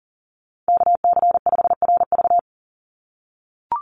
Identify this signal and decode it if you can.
Morse code, heard as a keyed tone, about 40 wpm, 700 Hz. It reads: KC5RV
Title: USA Call Signs 40wpm